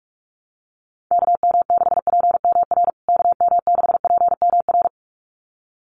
Morse code, heard as a keyed tone, about 44 wpm, 700 Hz. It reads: KM6PMR KM6PMR
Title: USA Call Signs Repeated 2x 45wpm